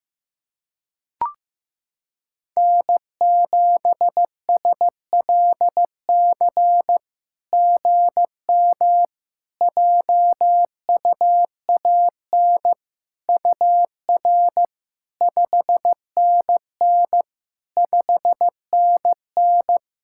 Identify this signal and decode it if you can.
Morse code, heard as a keyed tone, about 15 wpm, 700 Hz. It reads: N7SLC GM JUAN UR 5NN 5NN
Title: SOTA QSO Practice 15wpm